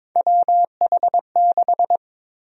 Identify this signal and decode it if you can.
Morse code, heard as a keyed tone, about 22 wpm, 700 Hz. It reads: WH6
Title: USA Call Sign Prefixes - 22wpm